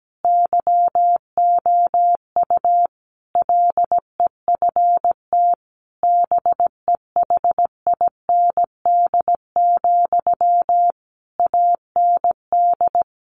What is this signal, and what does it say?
Morse code, heard as a keyed tone, about 17 wpm, 700 Hz. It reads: YOU LEFT BEHIND, AND
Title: Sentences from Top 1500 Words 17wpm